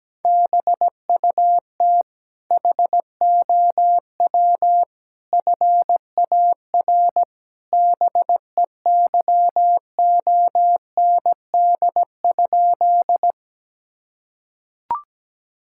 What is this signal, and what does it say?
Morse code, heard as a keyed tone, about 17 wpm, 700 Hz. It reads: BUT HOW FAR BEYOND?
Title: Sentences from Top 1000 Words - Review Part 6 - 17wpm